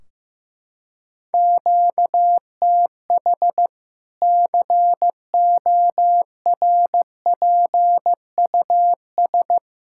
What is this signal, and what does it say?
Morse code, heard as a keyed tone, about 15 wpm, 700 Hz. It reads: QTH CORPUS
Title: Practice QSO - Exchange 1 - Beginning - 15wpm